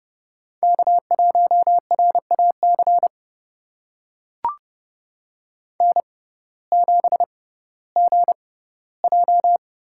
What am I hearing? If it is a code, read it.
Morse code, heard as a keyed tone, about 31 wpm, spaced as for 17 wpm, 700 Hz. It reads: K1RAC N7GJ
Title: USA Call Signs - 5 Characters - 6x Character Spacing - 30wpm